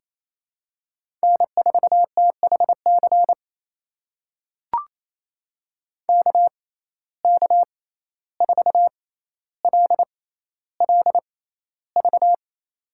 Morse code, heard as a keyed tone, about 28 wpm, spaced as for 14 wpm, 700 Hz. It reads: N4THC KK4LLV
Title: USA Call Signs - 6x Character Spacing - 28wpm